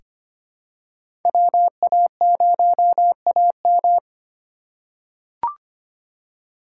Morse code, heard as a keyed tone, about 25 wpm, 700 Hz. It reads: WA0AM
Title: USA Call Signs - 5 Characters - 2x Character Spacing - 25wpm